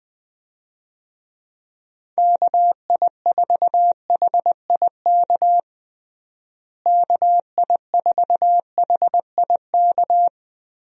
Morse code, heard as a keyed tone, about 20 wpm, 700 Hz. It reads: KI4HIK KI4HIK
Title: USA Call Signs - 6 Characters - Repeated 3x - 20wpm